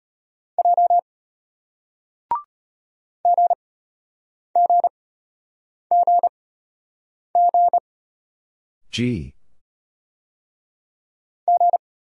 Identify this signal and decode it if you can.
Morse code, heard as a keyed tone, about 33 wpm, spaced as for 8 wpm, 700 Hz. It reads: J GGGG G